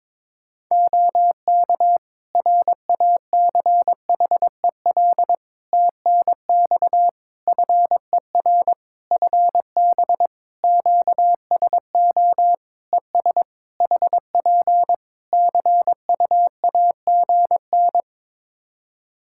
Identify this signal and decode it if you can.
Morse code, heard as a keyed tone, about 22 wpm, 700 Hz. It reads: OK RACHEL TNX FER FB QSO ES HP CUAGN
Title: K1USN SST QSO Practice 22wpm